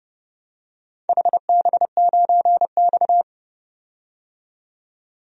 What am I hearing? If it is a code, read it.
Morse code, heard as a keyed tone, about 31 wpm, 700 Hz. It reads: HB9X
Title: DXCC-100 Call Signs - 4 Characters - 30wpm